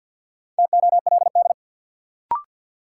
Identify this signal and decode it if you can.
Morse code, heard as a keyed tone, about 53 wpm, 700 Hz. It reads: TOLD
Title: Sentences from Top 400 Words - New Word Drill - 50wpm